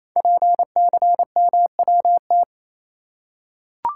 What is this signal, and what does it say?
Morse code, heard as a keyed tone, about 27 wpm, 700 Hz. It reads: PCMWT
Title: Code Groups - 5 Letters - 28wpm